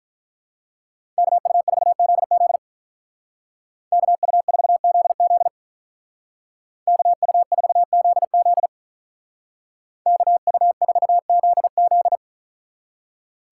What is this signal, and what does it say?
Morse code, heard as a keyed tone, about 43 wpm, 700 Hz. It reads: KU4ZZ KU4ZZ KU4ZZ KU4ZZ
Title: Speed Racing - USA Call Signs 35wpm